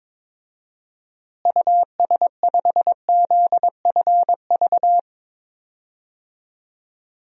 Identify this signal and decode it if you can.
Morse code, heard as a keyed tone, about 22 wpm, 700 Hz. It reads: US5ZFV
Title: DXCC-100 Call Signs - 6 Characters - Repeated 3x - 22wpm